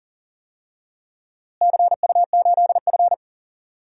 Morse code, heard as a keyed tone, about 40 wpm, 700 Hz. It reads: CU8F
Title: DXCC-100 Call Signs - 4 Characters - Repeated 2x - 40wpm